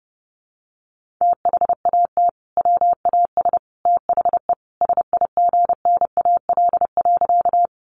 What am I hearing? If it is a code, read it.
Morse code, heard as a keyed tone, about 30 wpm, 700 Hz. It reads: THAT WAS THE SIGNAL.
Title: Sentences from Top 1400 Words 30wpm